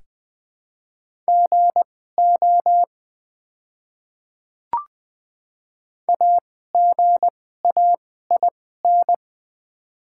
Morse code, heard as a keyed tone, about 20 wpm, spaced as for 15 wpm, 700 Hz. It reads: GO AGAIN
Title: 2x Character Spacing - Top 200 Words - 20wpm